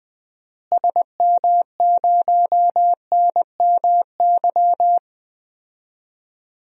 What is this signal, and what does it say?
Morse code, heard as a keyed tone, about 21 wpm, 700 Hz. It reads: SM0NMY
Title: DXCC-100 Call Signs - 6 Characters - Repeated 2x - 20wpm